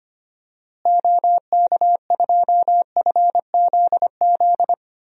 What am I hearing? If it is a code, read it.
Morse code, heard as a keyed tone, about 25 wpm, 700 Hz. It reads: OK2FZZ
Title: DXCC-100 Call Signs - 6 Characters - Repeated 3x - 25wpm